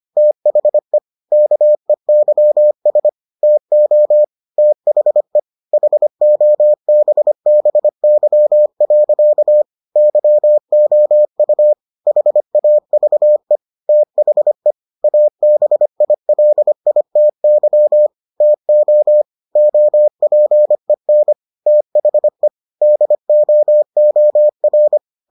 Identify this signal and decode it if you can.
Morse code, heard as a keyed tone, about 25 wpm, 600 Hz. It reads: THE KEYS TO THE HOBBY. YOU HAVE THE ABILITY TO OPEN THE DOOR